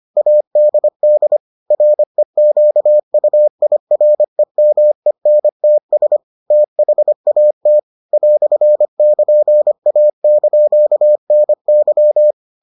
Morse code, heard as a keyed tone, about 25 wpm, 600 Hz. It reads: ADD REQUIREMENTS THAT "<KN>A)NY